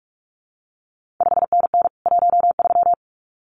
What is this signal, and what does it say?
Morse code, heard as a keyed tone, about 46 wpm, 700 Hz. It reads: 5NN 13
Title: CQ WW DX Contest Exchange Encore - 45wpm